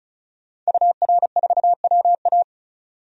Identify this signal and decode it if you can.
Morse code, heard as a keyed tone, about 35 wpm, 700 Hz. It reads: UR4WA
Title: DXCC-100 Call Signs - 5 Characters - 35wpm